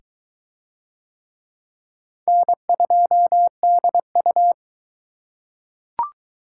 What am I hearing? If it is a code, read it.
Morse code, heard as a keyed tone, about 21 wpm, 700 Hz. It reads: N2DU